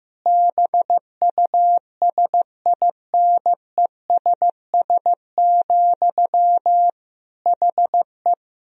Morse code, heard as a keyed tone, about 15 wpm, 700 Hz. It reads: BUSINESS, HE
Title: Sentences from Top 1000 Words - Review Part 3 - 15wpm